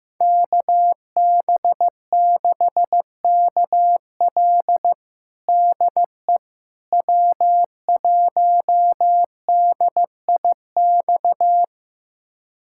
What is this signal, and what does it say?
Morse code, heard as a keyed tone, about 15 wpm, 700 Hz. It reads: KB6KL DE W1DIX
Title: USA Callsign DE Callsign 15wpm